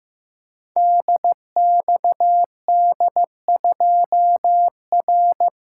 Morse code, heard as a keyed tone, about 15 wpm, 700 Hz. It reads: DXD2R
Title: Code Groups - 5 Letter:Numbers - 15wpm